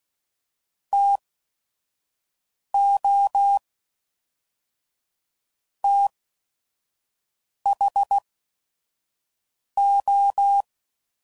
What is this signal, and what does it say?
Morse code, heard as a keyed tone, about 16 wpm, 800 Hz. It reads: T O T H O